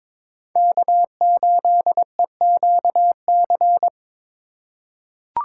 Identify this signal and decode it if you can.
Morse code, heard as a keyed tone, about 22 wpm, 700 Hz. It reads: K8EQC
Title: USA Call Signs - 5 Characters - 3x Character Spacing - 22wpm